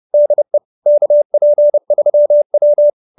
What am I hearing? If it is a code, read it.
Morse code, heard as a keyed tone, about 30 wpm, 590 Hz. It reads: DE KP3W
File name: QSO317 30